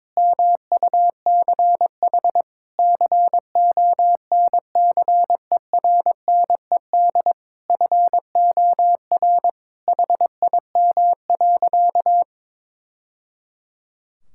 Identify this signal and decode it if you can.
Morse code, heard as a keyed tone, about 21 wpm, 700 Hz. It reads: MUCH CONCERNED FOR HIM.
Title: Sentences from Top 1000 Words - Review Part 6 - 22wpm